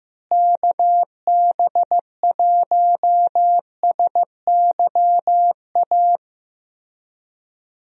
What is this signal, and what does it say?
Morse code, heard as a keyed tone, about 15 wpm, 700 Hz. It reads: KB1SYA
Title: USA Call Signs - 6 Characters - Repeated 2x - 15wpm